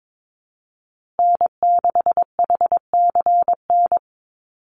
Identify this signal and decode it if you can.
Morse code, heard as a keyed tone, about 22 wpm, 700 Hz. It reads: N6HCN
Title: Call Signs Repeated 2x - 22wpm